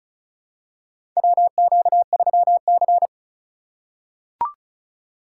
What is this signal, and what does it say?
Morse code, heard as a keyed tone, about 36 wpm, 700 Hz. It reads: WQ3C
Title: USA Call Signs - 4 Characters - 4x Character Spacing - 35wpm